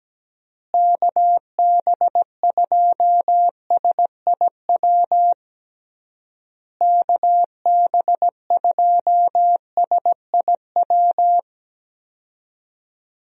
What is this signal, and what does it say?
Morse code, heard as a keyed tone, about 17 wpm, 700 Hz. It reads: KB2SIW KB2SIW
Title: USA Call Signs - 6 Characters - Repeated 3x - 17wpm